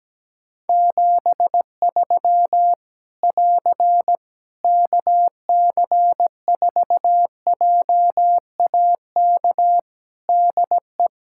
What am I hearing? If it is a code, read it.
Morse code, heard as a keyed tone, about 17 wpm, 700 Hz. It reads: 73 <AR> KC4JAK DE
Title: Practice QSO - Exchange 4 - Ending - 17wpm